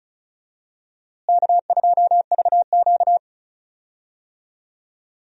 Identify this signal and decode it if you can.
Morse code, heard as a keyed tone, about 34 wpm, 700 Hz. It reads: K2VQ
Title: USA Call Signs - 4 Characters - Repeated 2x - 35wpm